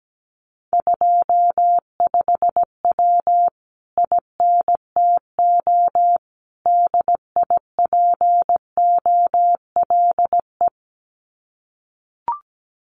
Morse code, heard as a keyed tone, about 17 wpm, 700 Hz. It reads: 25W INTO DIPOLE
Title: Common QSO Phrases Encore 17wpm